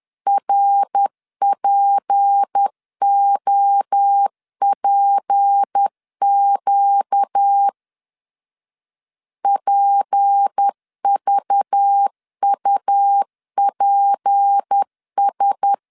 Morse code, heard as a keyed tone, about 10 wpm, 800 Hz. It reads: RPOPQ PVUPS
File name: LEZIONI / 6 PKX0 / 6 P 10wpm 2432